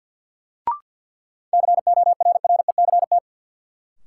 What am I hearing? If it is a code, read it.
Morse code, heard as a keyed tone, about 51 wpm, 700 Hz. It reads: CORRECT
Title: Sentences from Top 700 Words - New Word Drill - 50wpm